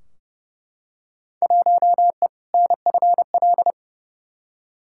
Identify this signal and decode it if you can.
Morse code, heard as a keyed tone, about 30 wpm, 700 Hz. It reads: 1E NFL
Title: Field Day Exchanges - 30wpm